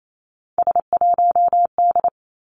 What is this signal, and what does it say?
Morse code, heard as a keyed tone, about 28 wpm, 700 Hz. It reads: S1D
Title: Three Challenging Letter-Number Combinations 28wpm